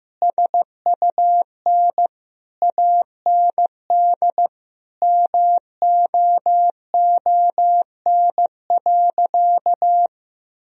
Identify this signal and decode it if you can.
Morse code, heard as a keyed tone, about 14 wpm, 700 Hz. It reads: SUN AND MOON.
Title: Sentences from Top 1000 Words - Review Part 3 - 15wpm